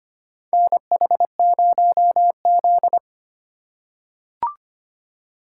Morse code, heard as a keyed tone, about 25 wpm, 700 Hz. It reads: NH0Z
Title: USA Call Signs - 4 Characters - 3x Character Spacing - 25wpm